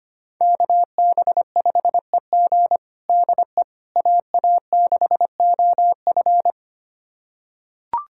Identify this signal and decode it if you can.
Morse code, heard as a keyed tone, about 25 wpm, 700 Hz. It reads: KB5EG DE AA6OF
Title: USA Callsign DE Callsign - 5 Characters - 25wpm